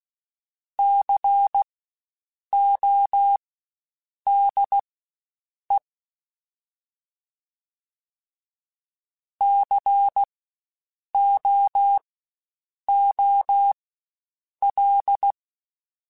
Morse code, heard as a keyed tone, about 16 wpm, spaced as for 7 wpm, 800 Hz. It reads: CODE COOL
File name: Spaced Alphas / Spaced Alphas mp3s / Cs sp @15